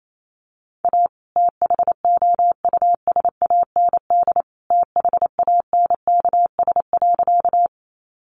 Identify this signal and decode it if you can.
Morse code, heard as a keyed tone, about 28 wpm, 700 Hz. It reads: A THOUSAND THANKS.
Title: Sentences from Top 600 Words 28wpm